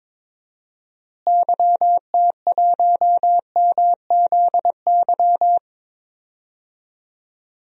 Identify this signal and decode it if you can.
Morse code, heard as a keyed tone, about 22 wpm, 700 Hz. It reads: YT1MZY
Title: DXCC-100 Call Signs - 6 Characters - Repeated 2x - 22wpm